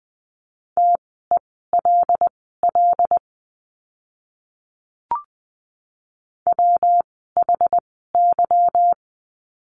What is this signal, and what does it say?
Morse code, heard as a keyed tone, about 20 wpm, spaced as for 14 wpm, 700 Hz. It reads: TELL WHY
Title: 192 Lesson 31f - Letter v - Words 20wpm